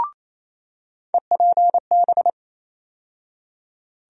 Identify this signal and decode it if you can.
Morse code, heard as a keyed tone, about 28 wpm, 700 Hz. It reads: EPB
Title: USA Call Sign Postfixes - 3 Letters 28wpm